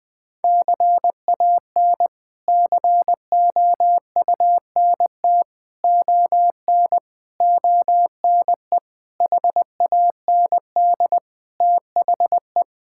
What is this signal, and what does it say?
Morse code, heard as a keyed tone, about 20 wpm, 700 Hz. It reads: CAN COUNT ON ONE HAND THE
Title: Sets of 8 Words - 20wpm